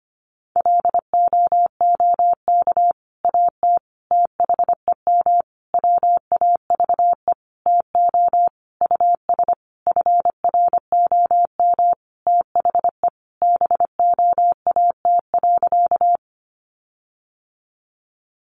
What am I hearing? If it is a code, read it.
Morse code, heard as a keyed tone, about 25 wpm, 700 Hz. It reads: LOOK AT THEM WAVE TO US FROM THE BOAT.
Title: Sentences from Top 1400 Words 25wpm